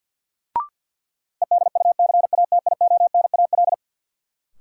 Morse code, heard as a keyed tone, about 53 wpm, 700 Hz. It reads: EDUCATIONAL
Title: Sentences from Top 1200 Words - New Word Drill - 50wpm